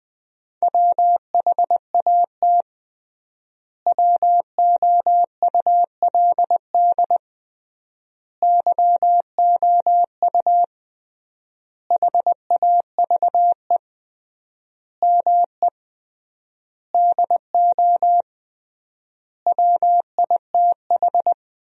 Morse code, heard as a keyed tone, about 20 wpm, 700 Hz. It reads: WHAT WOULD YOU HAVE ME DO WITH
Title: 3x Word-Spacing - Sentences from Top 300 Words - 20wpm